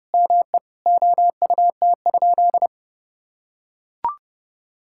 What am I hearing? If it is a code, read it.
Morse code, heard as a keyed tone, about 30 wpm, 700 Hz. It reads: ME OUT?
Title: Sentences from Top 1600 Words 30wpm